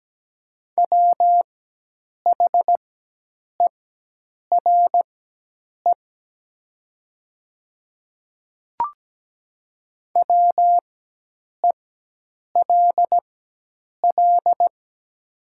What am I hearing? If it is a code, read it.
Morse code, heard as a keyed tone, about 17 wpm, spaced as for 8 wpm, 700 Hz. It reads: WHERE WELL
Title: 4x Character Spacing - Top 200 Words - 17wpm